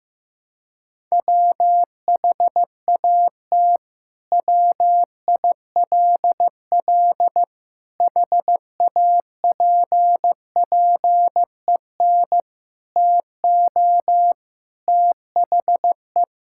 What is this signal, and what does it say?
Morse code, heard as a keyed tone, about 16 wpm, 700 Hz. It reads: WHAT WILL HAPPEN TO THE
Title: Sentences from Top 1000 Words - Review Part 5 - 15wpm